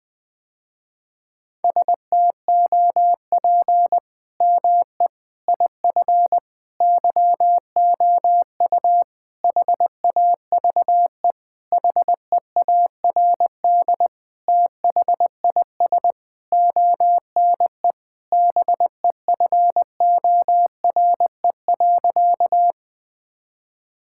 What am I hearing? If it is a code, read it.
Morse code, heard as a keyed tone, about 20 wpm, 700 Hz. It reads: STOP ME IF YOU HAVE HEARD THIS ONE BEFORE.
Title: Sentences with 8-Word Phrases - Spoken Beginning - 20wpm